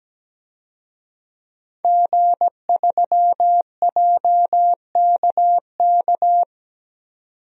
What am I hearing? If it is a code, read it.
Morse code, heard as a keyed tone, about 17 wpm, 700 Hz. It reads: G3JKK
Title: DXCC-100 Call Signs - 5 Characters - 17wpm